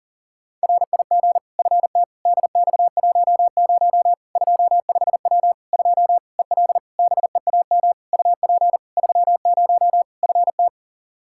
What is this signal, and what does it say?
Morse code, heard as a keyed tone, about 40 wpm, 700 Hz. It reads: RIG FT DX10 25W 2 EL BEAM UP 30 FT
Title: Practice QSO - Exchange 2 - Info - 40wpm